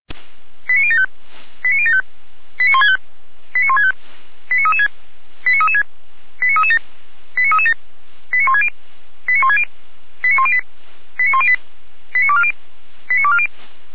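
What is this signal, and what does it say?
Five-tone selective call (ZVEI), heard as quick runs of five tones, tones of about 70 ms. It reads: 89075 89075 89165 89165 89336 89336 89337 89337 89160 89160 89108 89108 89240 89240
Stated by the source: ZVEI five-tone selective call received on an SDR.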